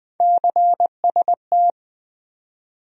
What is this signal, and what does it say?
Morse code, heard as a keyed tone, about 21 wpm, 700 Hz. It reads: CST
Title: USA Call Sign Postfixes - 3 Letters 20wpm